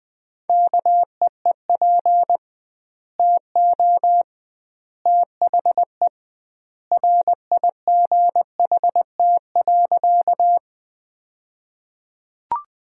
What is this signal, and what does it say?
Morse code, heard as a keyed tone, about 20 wpm, 700 Hz. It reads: KEEP TO THE RIGHT.
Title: 2x Word-Spacing - Sentences from Top 300 Words - 20wpm